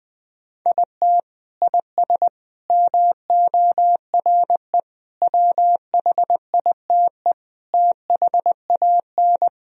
Morse code, heard as a keyed tone, about 21 wpm, 700 Hz. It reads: IT IS MORE WHITE THAN
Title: Sentences from Top 1000 Words - Review Part 1 - 20wpm